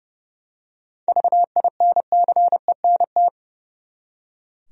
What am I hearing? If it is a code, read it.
Morse code, heard as a keyed tone, about 30 wpm, 700 Hz. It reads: VINCENT
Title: USA Common Names - 30wpm